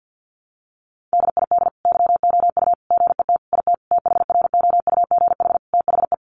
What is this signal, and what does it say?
Morse code, heard as a keyed tone, about 50 wpm, 700 Hz. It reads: DID YOU GET IT THROUGH THE
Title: Sentences from Top 1300 Words 50wpm